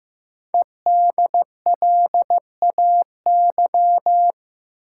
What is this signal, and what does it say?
Morse code, heard as a keyed tone, about 15 wpm, 700 Hz. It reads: EDLAY
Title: Code Groups - 5 Letters - 15wpm